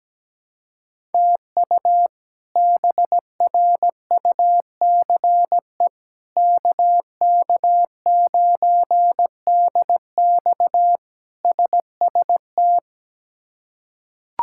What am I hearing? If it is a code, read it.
Morse code, heard as a keyed tone, about 17 wpm, 700 Hz. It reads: TU BRUCE KK9DX SST
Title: K1USN SST QSO Practice - 17wpm